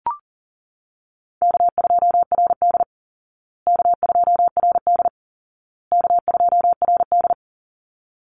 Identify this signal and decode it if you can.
Morse code, heard as a keyed tone, about 40 wpm, 700 Hz. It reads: K2RD K2RD K2RD
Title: Easy Call Signs Repeated 3x - 40wpm